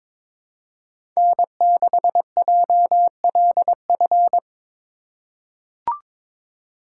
Morse code, heard as a keyed tone, about 21 wpm, 700 Hz. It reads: N6JLF
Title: USA Call Signs - 3x Character Spacing - 22wpm